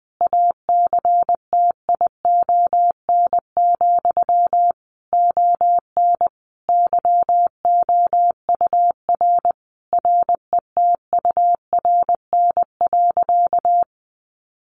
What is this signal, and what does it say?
Morse code, heard as a keyed tone, about 20 wpm, 700 Hz. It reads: ACTION, ON YOUR RETURN.
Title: A Sound Of Thunder 20wpm